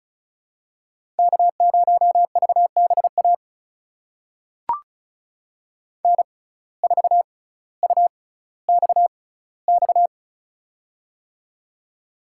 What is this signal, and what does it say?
Morse code, heard as a keyed tone, about 36 wpm, spaced as for 18 wpm, 700 Hz. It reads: K0VBA N4UXX
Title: USA Call Signs - 5 Characters - 6x Character Spacing - 35wpm